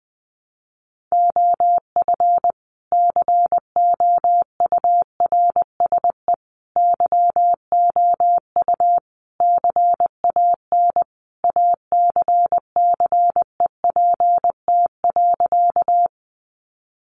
Morse code, heard as a keyed tone, about 20 wpm, 700 Hz. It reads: OF COURSE YOU CAN ACCEPT.
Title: Sentences from Top 1300 Words 20wpm